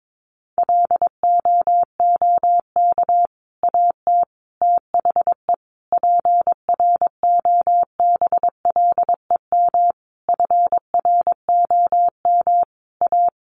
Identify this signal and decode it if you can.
Morse code, heard as a keyed tone, about 22 wpm, 700 Hz. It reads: LOOK AT THE PROBLEM FROM A
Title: Sentences from Top 1500 Words 22wpm